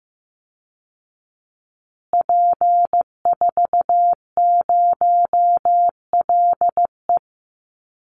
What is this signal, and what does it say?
Morse code, heard as a keyed tone, about 15 wpm, 700 Hz. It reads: P40LE
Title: Call Signs Repeated 3x - 15wpm